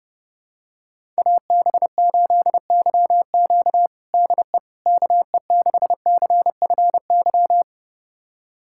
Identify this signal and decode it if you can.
Morse code, heard as a keyed tone, about 32 wpm, 700 Hz. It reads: AB8YQ DE KE6CFY
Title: USA Callsign DE Callsign 30wpm